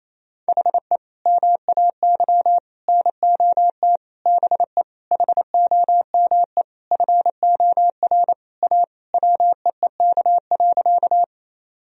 Morse code, heard as a keyed tone, about 28 wpm, 700 Hz. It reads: HE MAY NOT BE HOME FOR A WEEK.
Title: Sentences from Top 1000 Words - Review Part 2 - 28wpm